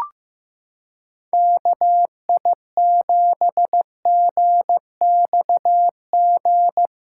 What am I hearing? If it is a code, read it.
Morse code, heard as a keyed tone, about 16 wpm, 700 Hz. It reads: KI7GXG
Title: USA Callsign DE Callsign 15wpm